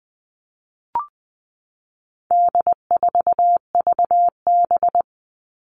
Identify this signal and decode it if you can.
Morse code, heard as a keyed tone, about 20 wpm, 700 Hz. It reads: D4VB
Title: All Cut Numbers - 4-Digit 20wpm